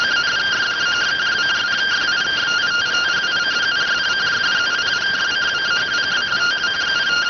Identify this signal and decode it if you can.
Two-tone teleprinter signal, about 170 Hz shift.